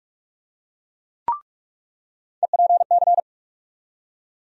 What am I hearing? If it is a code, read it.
Morse code, heard as a keyed tone, about 44 wpm, 700 Hz. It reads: EPC
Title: USA Call Sign Postfixes - 3 Letters 45wpm